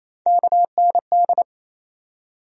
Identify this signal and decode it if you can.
Morse code, heard as a keyed tone, about 28 wpm, 700 Hz. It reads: KND
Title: USA Call Sign Postfixes - 3 Letters 28wpm